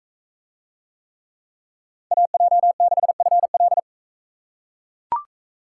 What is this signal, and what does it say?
Morse code, heard as a keyed tone, about 42 wpm, 700 Hz. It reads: AJ6FL